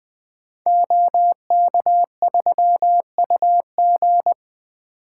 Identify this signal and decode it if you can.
Morse code, heard as a keyed tone, about 20 wpm, 700 Hz. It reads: OK3UG
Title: Code Groups - 5 Letter:Numbers - 20wpm